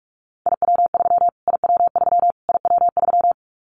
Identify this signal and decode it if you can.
Morse code, heard as a keyed tone, about 46 wpm, 700 Hz. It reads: IW3 IW3 IW3
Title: Callsign Prefixes for Top 25 DXCC - Repeated 3x - 45wpm